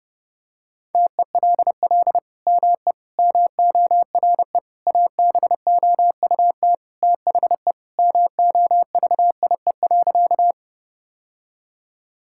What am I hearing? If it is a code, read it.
Morse code, heard as a keyed tone, about 30 wpm, 700 Hz. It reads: TELL ME MORE ABOUT THE MOVIE.
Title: Sentences from Top 1000 Words - Review Part 5 - 30wpm